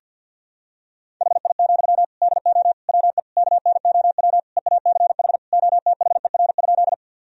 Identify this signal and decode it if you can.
Morse code, heard as a keyed tone, about 51 wpm, 700 Hz. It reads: HI, DO WE KNOW EACH OTHER?
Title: Sentences from Top 1000 Words - Review Part 3 - 50wpm